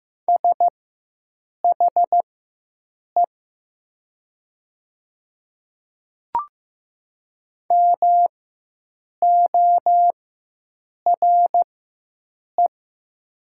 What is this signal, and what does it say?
Morse code, heard as a keyed tone, about 15 wpm, spaced as for 7 wpm, 700 Hz. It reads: SHE MORE
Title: 4x Character Spacing - Top 100 Words - 15wpm